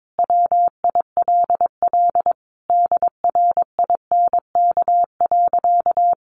Morse code, heard as a keyed tone, about 22 wpm, 700 Hz. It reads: WILL DRINK.
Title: Sentences from Top 1500 Words 22wpm